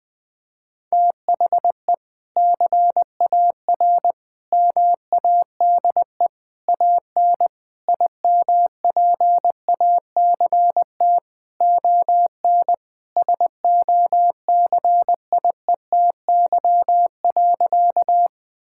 Morse code, heard as a keyed tone, about 21 wpm, 700 Hz. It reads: THE CAR MADE AN IMPACT ON SOCIETY.
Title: Sentences from Top 1000 Words - Review Part 6 - 20wpm